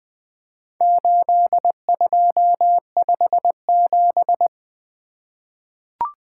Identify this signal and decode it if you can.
Morse code, heard as a keyed tone, about 20 wpm, 700 Hz. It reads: 8257
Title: Four Letter-Number Combinations - 20wpm